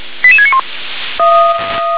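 Five-tone selective call (ZVEI), heard as quick runs of five tones, tones of about 70 ms. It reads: 88691
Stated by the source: ZVEI five-tone selective call received on SDR.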